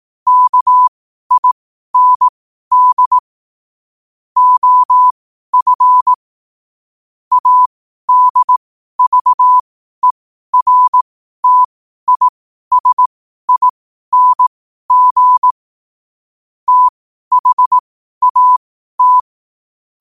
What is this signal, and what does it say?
Morse code, heard as a keyed tone, about 18 wpm, spaced as for 12 wpm, 1000 Hz. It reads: KIND OF ADVERTISING THAT
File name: qotdfiles / 12wpm